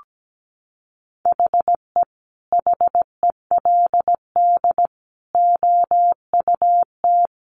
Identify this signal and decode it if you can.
Morse code, heard as a keyed tone, about 17 wpm, 700 Hz. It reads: HE HELD OUT
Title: A Sound Of Thunder 17wpm